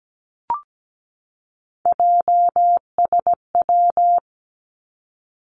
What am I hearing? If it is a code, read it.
Morse code, heard as a keyed tone, about 17 wpm, 700 Hz. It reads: JSW
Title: Three Challenging Letter-Number Combinations 17wpm